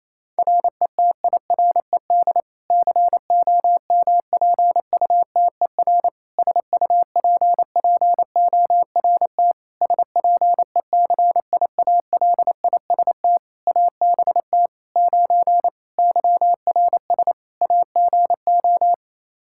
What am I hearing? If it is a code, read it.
Morse code, heard as a keyed tone, about 28 wpm, 700 Hz. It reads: RETIRED COMPUTER SUPPORT SPECIALIST ABT 9 YRS AGO
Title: Practice QSO - Exchange 3 - More Info - 28wpm